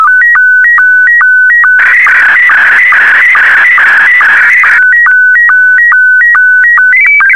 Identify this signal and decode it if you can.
SSTV (slow-scan television), Scottie 1.